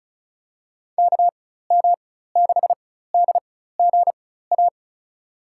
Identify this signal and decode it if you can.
Morse code, heard as a keyed tone, about 36 wpm, spaced as for 16 wpm, 700 Hz. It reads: KM6DGA
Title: USA Call Signs - 6 Characters - 4x Character Spacing - 35wpm